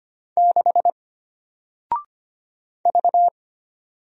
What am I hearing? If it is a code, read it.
Morse code, heard as a keyed tone, about 25 wpm, 700 Hz. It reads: 6 V
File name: Single-Letter-Number-25wpm